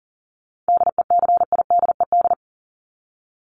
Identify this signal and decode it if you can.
Morse code, heard as a keyed tone, about 40 wpm, 700 Hz. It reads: DECIDED
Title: ED Suffix - Top 100 Words with Base Words 40wpm